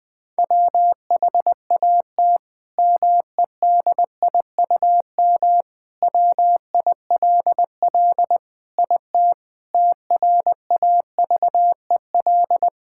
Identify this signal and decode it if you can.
Morse code, heard as a keyed tone, about 21 wpm, 700 Hz. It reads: WHAT MEDIUM WILL IT TRAVEL
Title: Sentences from Top 1000 Words - Review Part 6 - 20wpm